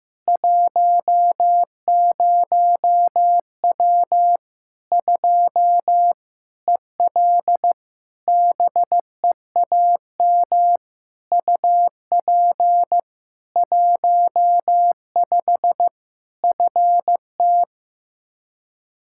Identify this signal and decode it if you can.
Morse code, heard as a keyed tone, about 15 wpm, 700 Hz. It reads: 10W 2 EL BEAM UP 15 FT
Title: Practice QSO - Exchange 2 - Info - 15wpm